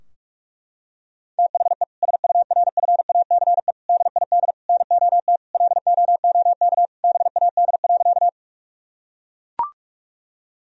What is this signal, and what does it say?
Morse code, heard as a keyed tone, about 42 wpm, 700 Hz. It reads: THE SURFACE DID NOT LOOK BAD.
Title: Sentences from Top 1000 Words - Review Part 3 - 45wpm